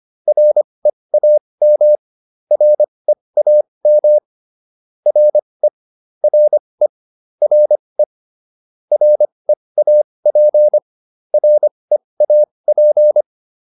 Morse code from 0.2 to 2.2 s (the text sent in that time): REAM